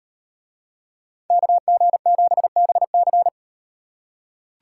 Morse code, heard as a keyed tone, about 37 wpm, 700 Hz. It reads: KG7BC